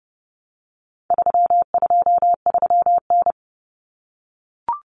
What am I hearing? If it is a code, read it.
Morse code, heard as a keyed tone, about 30 wpm, 700 Hz. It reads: 323N
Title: Common Cut Numbers - 4-Digit 30wpm